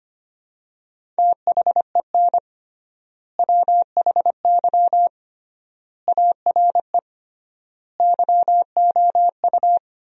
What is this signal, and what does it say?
Morse code, heard as a keyed tone, about 25 wpm, 700 Hz. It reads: THEN WHY ARE YOU
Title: 3x Word-Spacing - Sentences from Top 200 Words - 25wpm